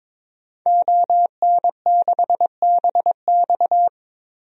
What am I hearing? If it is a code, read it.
Morse code, heard as a keyed tone, about 22 wpm, 700 Hz. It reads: ON6BX
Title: DXCC-100 Call Signs - 5 Characters - 22wpm